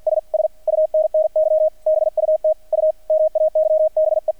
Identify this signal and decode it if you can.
Morse code, heard as a keyed tone, about 48 wpm, 620 Hz, with radio noise. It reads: S SUNNY BUT U MAYBE
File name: archived / CRSnet